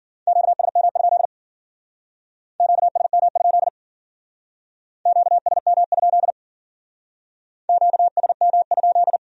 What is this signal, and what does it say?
Morse code, heard as a keyed tone, about 50 wpm, 700 Hz. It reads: QSM? QSM? QSM? QSM?
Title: Speed-Racing Q-Codes - 40wpm